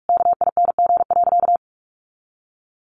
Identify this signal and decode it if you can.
Morse code, heard as a keyed tone, about 44 wpm, 700 Hz. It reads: KING.
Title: Sentences from Top 400 Words 45wpm